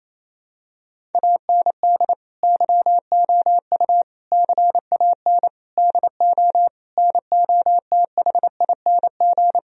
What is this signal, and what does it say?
Morse code, heard as a keyed tone, about 28 wpm, 700 Hz. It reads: AND YOU CAN DO NOTHING
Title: Sentences from Top 1000 Words - Review Part 1 - 28wpm